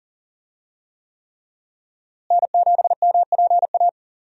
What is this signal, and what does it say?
Morse code, heard as a keyed tone, about 40 wpm, 700 Hz. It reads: N7MPA
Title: USA Call Signs - 5 Characters - Repeated 3x - 40wpm